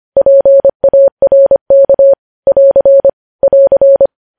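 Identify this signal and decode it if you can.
Morse code, heard as a keyed tone, about 25 wpm, 550 Hz. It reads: PARK <AR> <AR>
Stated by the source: News headlines sent in Morse at 25 wpm.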